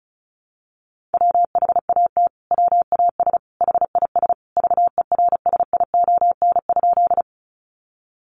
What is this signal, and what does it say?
Morse code, heard as a keyed tone, about 34 wpm, 700 Hz. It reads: WHAT WAS HIS VERSION?
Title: Sentences from Top 1400 Words 35wpm